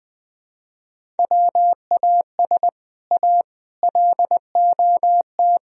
Morse code, heard as a keyed tone, about 20 wpm, 700 Hz. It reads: WAS A LOT
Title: Speed-Racing - Sets of 4 Words 17wpm